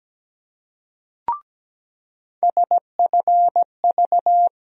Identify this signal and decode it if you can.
Morse code, heard as a keyed tone, about 17 wpm, 700 Hz. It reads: SFV
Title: USA Call Sign Postfixes - 3 Letters 17wpm